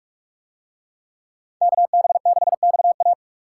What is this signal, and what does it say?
Morse code, heard as a keyed tone, about 44 wpm, 700 Hz. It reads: KB6XA